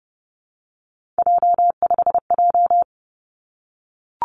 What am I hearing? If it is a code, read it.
Morse code, heard as a keyed tone, about 32 wpm, 700 Hz. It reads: J5J
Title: Three Challenging Letter-Number Combinations 30wpm